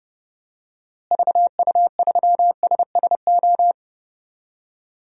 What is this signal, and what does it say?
Morse code, heard as a keyed tone, about 30 wpm, 700 Hz. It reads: VU3SSO
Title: DXCC-100 Call Signs - 6 Characters - Repeated 2x - 30wpm